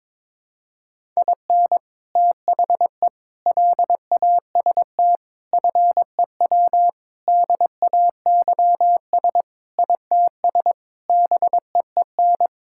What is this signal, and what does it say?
Morse code, heard as a keyed tone, about 22 wpm, 700 Hz. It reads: IN THE LAST FEW DAYS ITS BEEN
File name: practice / 22wpm / Sentences with 3-Word Phrases - 22wpm